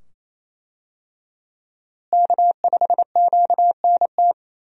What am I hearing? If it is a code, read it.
Morse code, heard as a keyed tone, about 28 wpm, 700 Hz. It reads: K5QNT
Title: USA Call Signs - 5 Characters - Repeated 2x - 28wpm